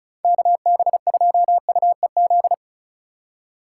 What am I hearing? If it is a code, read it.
Morse code, heard as a keyed tone, about 36 wpm, 700 Hz. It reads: KB2UEZ
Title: USA Call Signs - 6 Characters - 4x Character Spacing - 35wpm